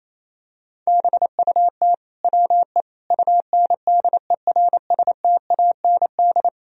Morse code, heard as a keyed tone, about 28 wpm, 700 Hz. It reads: BUT WE UNDERSTAND
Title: Sentences from Top 1000 Words - Review Part 1 - 28wpm